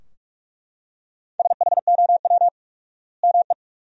Morse code, heard as a keyed tone, about 43 wpm, 700 Hz. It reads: SHOW ME
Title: 4x Word-Spacing - Sentences from Top 200 Words - 45wpm